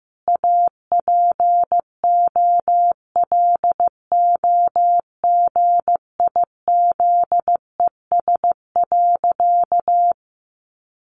Morse code, heard as a keyed tone, about 15 wpm, 700 Hz. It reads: APOLOGIZES.
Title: Additional Easy Phrases 15wpm